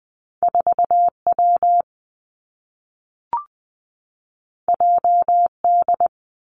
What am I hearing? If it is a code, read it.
Morse code, heard as a keyed tone, about 20 wpm, 700 Hz. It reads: 4 W J D
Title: Two Challenging Letter-Number Combinations 20wpm